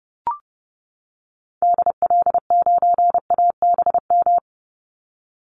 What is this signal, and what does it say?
Morse code, heard as a keyed tone, about 30 wpm, 700 Hz. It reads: DL9ABM
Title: Call Signs - 30wpm